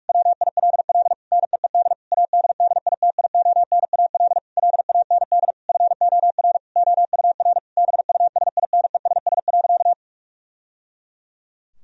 Morse code, heard as a keyed tone, about 46 wpm, 700 Hz. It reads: WILL NEED ADDITIONAL LAND FOR OUR BUSINESS.
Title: Sentences from Top 1000 Words - Review Part 4 - 45wpm